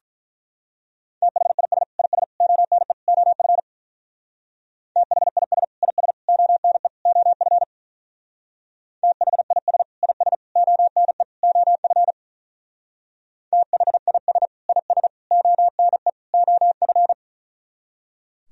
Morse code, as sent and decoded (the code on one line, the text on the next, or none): - .... .. ... / .. ... / --- -. / --- ..-. / - .... .. ... / .. ... / --- -. / --- ..-. / - .... .. ... / .. ... / --- -. / --- ..-. / - .... .. ... / .. ... / --- -. / --- ..-.
THIS IS ONE OF THIS IS ONE OF THIS IS ONE OF THIS IS ONE OF